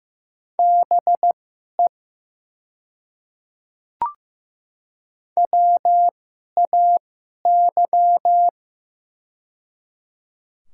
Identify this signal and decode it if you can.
Morse code, heard as a keyed tone, about 15 wpm, spaced as for 11 wpm, 700 Hz. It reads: BE WAY
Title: 2x Character Spacing - Top 100 Words - 15wpm